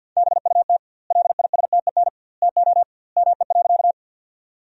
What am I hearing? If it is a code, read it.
Morse code, heard as a keyed tone, about 50 wpm, 700 Hz. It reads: BUT LISTEN TO ME.
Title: 2x Word-Spacing - Sentences from Top 400 Words - 50wpm